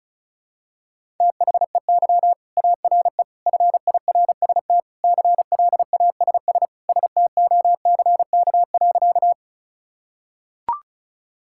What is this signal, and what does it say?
Morse code, heard as a keyed tone, about 36 wpm, 700 Hz. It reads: THEY ARE FIRST CLASS STOCK.
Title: Sentences from Top 1000 Words - Review Part 5 - 35wpm